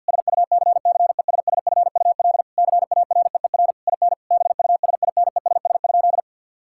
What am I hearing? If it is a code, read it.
Morse code, heard as a keyed tone, about 50 wpm, 700 Hz. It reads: SUCCESSFUL CAREER IN BUSINESS?
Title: Sentences from Top 1000 Words - Review Part 6 - 50wpm